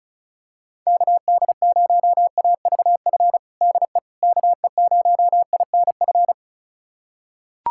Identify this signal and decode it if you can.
Morse code, heard as a keyed tone, about 34 wpm, 690 Hz. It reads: KD0AVF DE KE0INF
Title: USA Callsign DE Callsign 35wpm